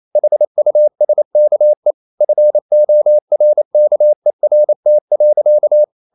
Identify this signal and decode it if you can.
Morse code, heard as a keyed tone, about 28 wpm, 600 Hz. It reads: HUSKE FORKERT.
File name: Ferie QRQ 28wpm